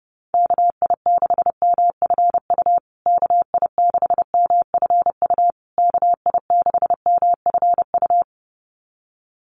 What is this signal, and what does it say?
Morse code, heard as a keyed tone, about 30 wpm, 700 Hz. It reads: KI6MFU KI6MFU KI6MFU
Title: USA Call Signs Repeated 3x 30wpm